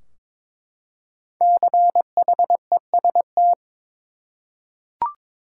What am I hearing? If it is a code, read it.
Morse code, heard as a keyed tone, about 21 wpm, 700 Hz. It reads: CHEST
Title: Five Letter Words - 22wpm